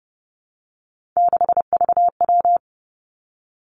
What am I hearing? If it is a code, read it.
Morse code, heard as a keyed tone, about 30 wpm, 700 Hz. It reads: 6VW
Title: Three Challenging Letter-Number Combinations 30wpm